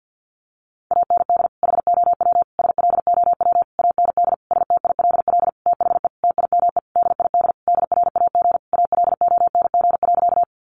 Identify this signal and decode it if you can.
Morse code, heard as a keyed tone, about 50 wpm, 700 Hz. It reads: AND HOW SLOW AND STILL THE TIME DID DRAG ALONG.
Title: Sentences from Top 1500 Words 50wpm